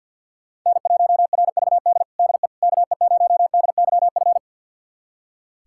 Morse code, heard as a keyed tone, about 46 wpm, 700 Hz. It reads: N1RVD DE KE0DYF
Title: USA Callsign DE Callsign 50wpm